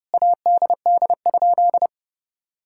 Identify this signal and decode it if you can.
Morse code, heard as a keyed tone, about 30 wpm, 700 Hz. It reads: ADD?
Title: Sentences from Top 1000 Words - Review Part 5 - 30wpm